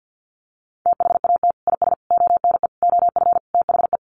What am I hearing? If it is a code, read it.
Morse code, heard as a keyed tone, about 50 wpm, 700 Hz. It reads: THAT IS ONE OF THE
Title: Sets of 5 Words Encore 50wpm